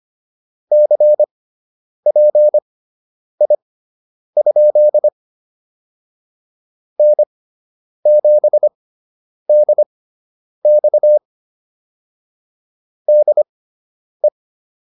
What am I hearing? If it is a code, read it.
Morse code, heard as a keyed tone, about 25 wpm, spaced as for 9 wpm, 600 Hz. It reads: CPI? N7DX DE